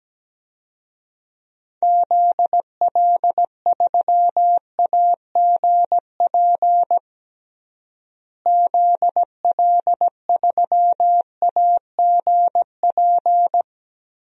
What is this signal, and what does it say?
Morse code, heard as a keyed tone, about 17 wpm, 700 Hz. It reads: ZL3AGP ZL3AGP
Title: DXCC-100 Call Signs - 6 Characters - Repeated 2x - 17wpm